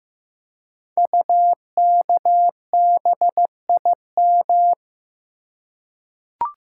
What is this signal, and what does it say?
Morse code, heard as a keyed tone, about 15 wpm, 700 Hz. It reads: UKBIM
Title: Code Groups - 5 Letters - 15wpm